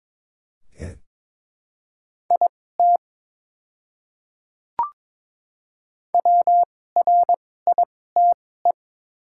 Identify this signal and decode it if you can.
Morse code, heard as a keyed tone, about 22 wpm, spaced as for 16 wpm, 700 Hz. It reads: IT WRITE
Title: 2x Character Spacing - Top 100 Words - 22wpm